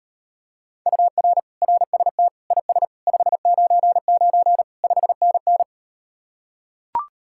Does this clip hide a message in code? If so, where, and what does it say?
Yes: Morse code from 0.9 to 5.6 s, heard as a keyed tone, about 38 wpm, 700 Hz. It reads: UR RST IS 599 5NN